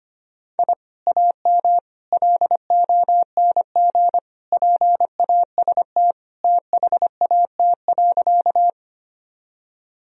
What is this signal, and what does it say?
Morse code, heard as a keyed tone, about 25 wpm, 700 Hz. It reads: I AM LONG PAST THAT.
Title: Sentences from Top 1000 Words - Review Part 3 - 25wpm